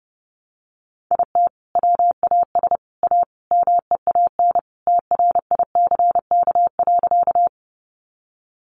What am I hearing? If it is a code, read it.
Morse code, heard as a keyed tone, about 30 wpm, 700 Hz. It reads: IT WAS A MEAN TRICK.
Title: Sentences from Top 1500 Words 30wpm